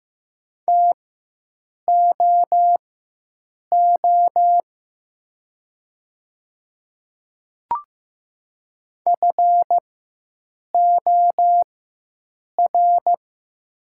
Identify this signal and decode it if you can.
Morse code, heard as a keyed tone, about 15 wpm, spaced as for 7 wpm, 700 Hz. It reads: TOO FOR
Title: 4x Character Spacing - Top 500 Words - 15wpm